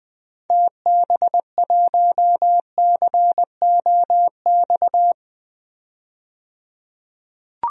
Morse code, heard as a keyed tone, about 21 wpm, 700 Hz. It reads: TB1COX
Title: DXCC-100 Call Signs - 6 Characters - 20wpm